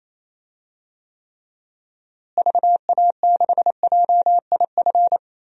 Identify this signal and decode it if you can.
Morse code, heard as a keyed tone, about 28 wpm, 700 Hz. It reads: VA6JIF
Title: DXCC-100 Call Signs - 6 Characters - Repeated 3x - 28wpm